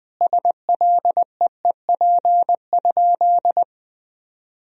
Morse code, heard as a keyed tone, about 20 wpm, 700 Hz. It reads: SLEEP?
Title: Sentences from Top 500 Words and Abbreviations 20wpm